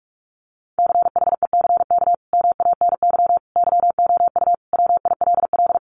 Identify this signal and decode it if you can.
Morse code, heard as a keyed tone, about 44 wpm, 700 Hz. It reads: CHECK MANY YOU WILL
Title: Sentences from Top 1500 Words 45wpm